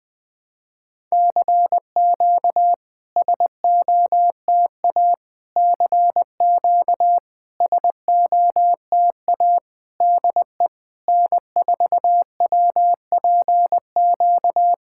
Morse code, heard as a keyed tone, about 20 wpm, 700 Hz. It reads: CQ SOTA CQ SOTA DE N4WPQ
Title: SOTA QSO Practice 20wpm